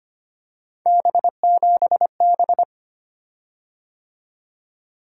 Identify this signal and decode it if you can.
Morse code, heard as a keyed tone, about 25 wpm, 700 Hz. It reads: B7B
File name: Three Letter-Number Combinations - 25wpm